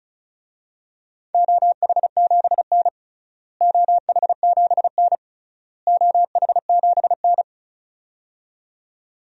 Morse code, heard as a keyed tone, about 34 wpm, 700 Hz. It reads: OH7N OH7N OH7N
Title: DXCC-100 Call Signs - 4 Characters - Repeated 3x - 35wpm